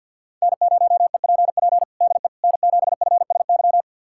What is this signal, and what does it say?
Morse code, heard as a keyed tone, about 52 wpm, 700 Hz. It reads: N0EPP DE N7FSY
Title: USA Callsign DE Callsign - 5 Characters - 50wpm